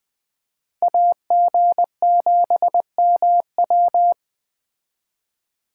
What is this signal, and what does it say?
Morse code, heard as a keyed tone, about 21 wpm, 700 Hz. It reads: AG7MW
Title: Speed Racing - USA 2x2 Call Signs 17wpm